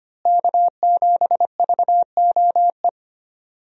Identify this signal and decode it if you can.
Morse code, heard as a keyed tone, about 25 wpm, 700 Hz. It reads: K7VOE
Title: USA Call Signs - 5 Characters - Repeated 2x - 25wpm